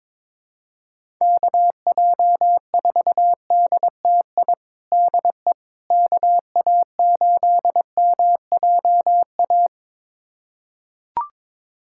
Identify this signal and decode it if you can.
Morse code, heard as a keyed tone, about 22 wpm, 700 Hz. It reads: KJ4DTI DE KA8MJA
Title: USA Callsign DE Callsign 22wpm